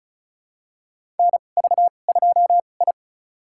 Morse code, heard as a keyed tone, about 36 wpm, spaced as for 25 wpm, 700 Hz. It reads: NV2I
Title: USA Call Signs - 4 Characters - 2x Character Spacing - 35wpm